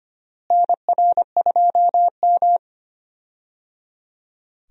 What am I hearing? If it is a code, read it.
Morse code, heard as a keyed tone, about 25 wpm, 700 Hz. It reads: NR2M
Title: USA Call Signs - 4 Characters - Repeated 3x - 25wpm